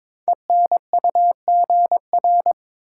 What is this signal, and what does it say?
Morse code, heard as a keyed tone, about 22 wpm, 700 Hz. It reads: ENUGR
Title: Code Groups - 5 Letters - 22wpm